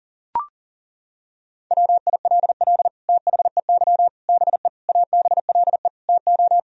Morse code, heard as a keyed tone, about 40 wpm, 700 Hz. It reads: WILL THEY BE ABLE TO
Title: Sentences from Top 1000 Words - Review Part 2 - 40wpm